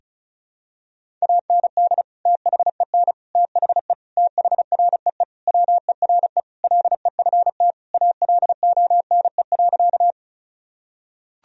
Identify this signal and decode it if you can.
Morse code, heard as a keyed tone, about 36 wpm, 700 Hz. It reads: AND THEN THE THREE WERE LEFT ALONE.
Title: Sentences from Top 1000 Words - Review Part 1 - 35wpm